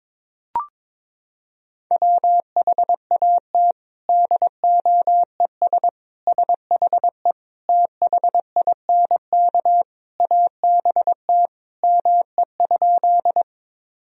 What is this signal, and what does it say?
Morse code, heard as a keyed tone, about 22 wpm, 700 Hz. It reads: WHAT DOES SHE THINK ABT ME?
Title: Sentences from Top 200 Words and Abbreviations 22wpm